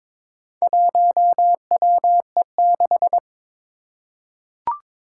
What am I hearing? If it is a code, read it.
Morse code, heard as a keyed tone, about 23 wpm, 700 Hz. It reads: 1WE6
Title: Four Challenging Letter-Number Combinations 22wpm